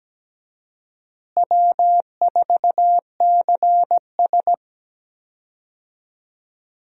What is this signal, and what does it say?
Morse code, heard as a keyed tone, about 17 wpm, 700 Hz. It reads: W4CS
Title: USA Call Signs - 4 Characters - Repeated 2x - 17wpm